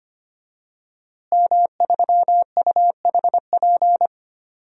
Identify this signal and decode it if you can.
Morse code, heard as a keyed tone, about 25 wpm, 700 Hz. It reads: M3UHP
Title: DXCC-100 Call Signs - 5 Characters - Repeated 2x - 25wpm